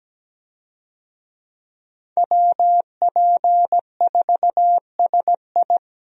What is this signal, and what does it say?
Morse code, heard as a keyed tone, about 17 wpm, 700 Hz. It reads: WP4SI
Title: USA Call Signs - 5 Characters - Repeated 3x - 17wpm